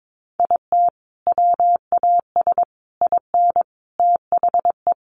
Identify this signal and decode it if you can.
Morse code, heard as a keyed tone, about 22 wpm, 700 Hz. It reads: IT WAS IN THE
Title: Sentences from Top 1400 Words 22wpm